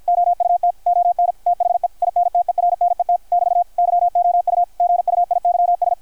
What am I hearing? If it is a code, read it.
Morse code, heard as a keyed tone, about 53 wpm, 720 Hz, with radio noise. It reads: OUT ON THE INTERNET <BT> YOU GUIYS
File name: archived / CRSnet